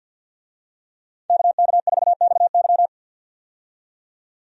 Morse code, heard as a keyed tone, about 51 wpm, 700 Hz. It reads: KK4XY
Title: USA Call Signs - 5 Characters - 50wpm